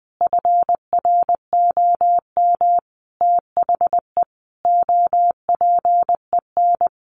Morse code, heard as a keyed tone, about 20 wpm, 700 Hz. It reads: FROM THE OPEN
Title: Sentences from Top 1000 Words 20wpm